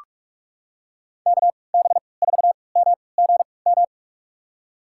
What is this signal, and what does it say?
Morse code, heard as a keyed tone, about 46 wpm, spaced as for 26 wpm, 700 Hz. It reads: KB4MGM
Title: USA Call Signs - 6 Characters - 3x Character Spacing - 45wpm